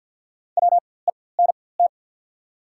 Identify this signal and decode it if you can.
Morse code, heard as a keyed tone, about 48 wpm, spaced as for 23 wpm, 700 Hz. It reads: WENT
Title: 4x Character Spacing - Top 500 Words - 50wpm